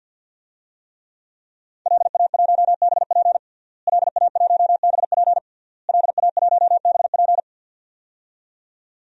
Morse code, heard as a keyed tone, about 46 wpm, 700 Hz. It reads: LA1BP LA1BP LA1BP
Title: DXCC-100 Call Signs - 5 Characters - Repeated 3x - 50wpm